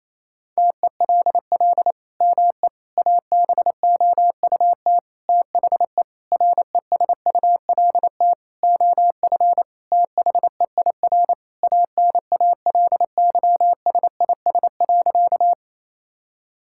Morse code, heard as a keyed tone, about 28 wpm, 700 Hz. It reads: TELL ME ABOUT THE RESULT OF THEIR ANALYSIS.
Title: Sentences from Top 1000 Words - Review Part 5 - 28wpm